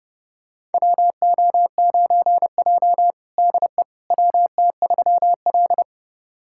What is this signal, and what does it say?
Morse code, heard as a keyed tone, about 30 wpm, 700 Hz. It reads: WO9J DE WT3L
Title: USA Callsign DE Callsign - 4 Characters - 30wpm